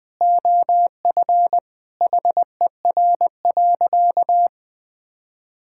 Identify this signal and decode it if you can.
Morse code, heard as a keyed tone, about 21 wpm, 700 Hz. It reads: OF HER.
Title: Sentences from Top 1000 Words - Review Part 5 - 20wpm